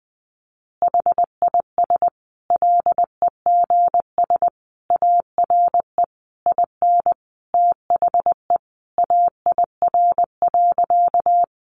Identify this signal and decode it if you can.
Morse code, heard as a keyed tone, about 20 wpm, 700 Hz. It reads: HIS LEGS ARE IN THE AIR.
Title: Additional Easy Phrases 20wpm